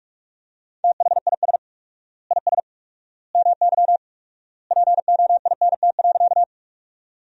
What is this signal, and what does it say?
Morse code, heard as a keyed tone, about 46 wpm, 700 Hz. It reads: THIS IS MY POINT.
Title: 4x Word-Spacing - Sentences from Top 200 Words - 45wpm